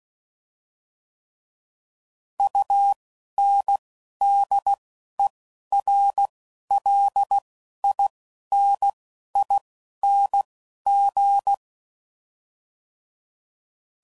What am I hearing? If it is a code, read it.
Morse code, heard as a keyed tone, about 16 wpm, 800 Hz. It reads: U N D E R L I N I N G